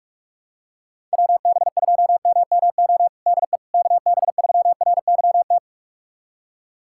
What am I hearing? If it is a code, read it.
Morse code, heard as a keyed tone, about 43 wpm, 700 Hz. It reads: WB2MMO DE KB3RYT
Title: USA Callsign DE Callsign - 6 Characters - 45wpm